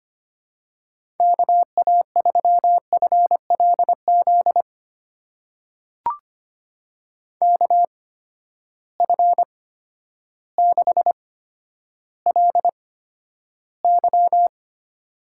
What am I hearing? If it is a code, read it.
Morse code, heard as a keyed tone, about 25 wpm, spaced as for 11 wpm, 700 Hz. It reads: KA3FLZ KF6LY
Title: USA Call Signs - 8x Character Spacing - 25wpm